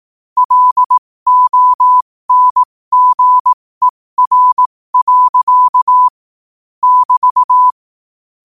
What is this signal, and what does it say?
Morse code, heard as a keyed tone, about 18 wpm, spaced as for 15 wpm, 1000 Hz. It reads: LONGER. <BT>
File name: qotdfiles / 15wpm